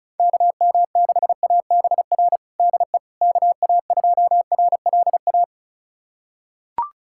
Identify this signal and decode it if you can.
Morse code, heard as a keyed tone, about 36 wpm, 700 Hz. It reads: KM6ABR DE KA2RLA
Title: USA Callsign DE Callsign 35wpm